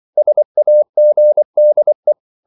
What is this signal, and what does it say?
Morse code, heard as a keyed tone, about 24 wpm, 600 Hz. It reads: SAGDE
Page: Ferie 24 wpm